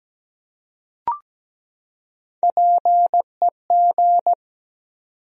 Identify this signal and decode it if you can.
Morse code, heard as a keyed tone, about 17 wpm, 700 Hz. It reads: PEG
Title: USA Common Names - Max 3 Letters - 17wpm